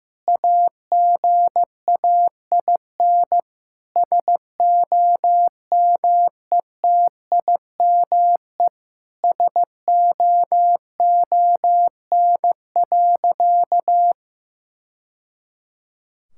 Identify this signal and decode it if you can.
Morse code, heard as a keyed tone, about 15 wpm, 700 Hz. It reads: AGAIN SOMETIME SOON.
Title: Sentences from Top 1000 Words - Review Part 6 - 15wpm